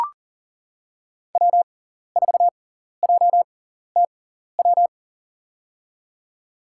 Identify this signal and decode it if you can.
Morse code, heard as a keyed tone, about 40 wpm, spaced as for 14 wpm, 700 Hz. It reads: W4JTW
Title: USA Call Signs - 5 Characters - 6x Character Spacing - 40wpm